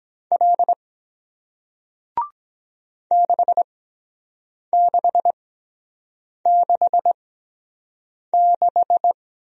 Morse code, heard as a keyed tone, about 23 wpm, spaced as for 7 wpm, 700 Hz. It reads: L 6666